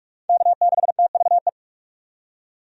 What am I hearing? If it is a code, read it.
Morse code, heard as a keyed tone, about 44 wpm, 700 Hz. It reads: K6TVE
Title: USA Call Signs - 5 Characters - 45wpm